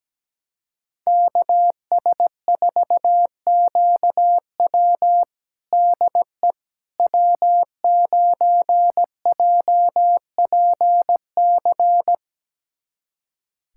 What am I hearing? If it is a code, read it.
Morse code, heard as a keyed tone, about 17 wpm, 700 Hz. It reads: KS4QW DE W9JPC
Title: USA Callsign DE Callsign - 5 Characters - 17wpm